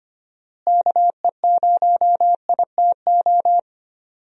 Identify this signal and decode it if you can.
Morse code, heard as a keyed tone, about 25 wpm, 700 Hz. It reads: KE0ITO
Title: USA Call Signs - 6 Characters - Repeated 2x - 25wpm